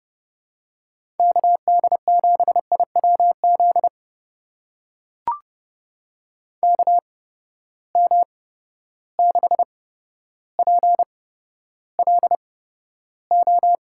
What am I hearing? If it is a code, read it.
Morse code, heard as a keyed tone, about 30 wpm, spaced as for 13 wpm, 700 Hz. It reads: KD7IWZ KM6PLO
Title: USA Call Signs - 6 Characters - 8x Character Spacing - 30wpm